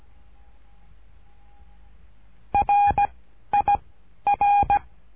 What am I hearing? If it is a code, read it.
Morse code, heard as a keyed tone, about 17 wpm, 810 Hz. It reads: R I R
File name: cursos / CURSO DE TELEGRAFIA / Telegrafia - Trilha 13